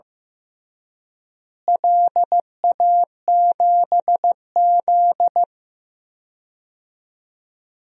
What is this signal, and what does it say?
Morse code, heard as a keyed tone, about 15 wpm, 700 Hz. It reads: LA7Z
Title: DXCC-100 Call Signs - 4 Characters - Repeated 3x - 15wpm